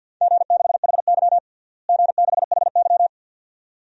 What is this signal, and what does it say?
Morse code, heard as a keyed tone, about 47 wpm, 700 Hz. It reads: G6HY G6HY
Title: DXCC-100 Call Signs - 4 Characters - Repeated 2x - 50wpm